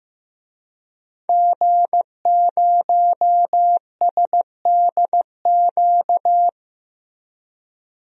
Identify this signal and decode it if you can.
Morse code, heard as a keyed tone, about 15 wpm, 700 Hz. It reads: G0SDQ
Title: DXCC-100 Call Signs - 5 Characters - Repeated 3x - 15wpm